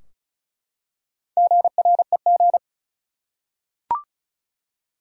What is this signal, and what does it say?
Morse code, heard as a keyed tone, about 35 wpm, 700 Hz. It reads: GREG
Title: USA Common Names - Max 4 Letters - 35wpm